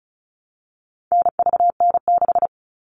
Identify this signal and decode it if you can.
Morse code, heard as a keyed tone, about 34 wpm, 700 Hz. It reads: NVN6
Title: All Cut Numbers - 4-Digit 35wpm